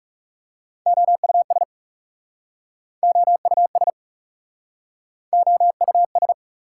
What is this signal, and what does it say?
Morse code, heard as a keyed tone, about 42 wpm, 690 Hz. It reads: OUS OUS OUS